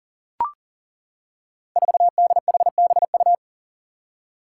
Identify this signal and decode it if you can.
Morse code, heard as a keyed tone, about 40 wpm, 700 Hz. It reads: 4DHBU
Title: Five Challenging Letter-Number Combinations 40wpm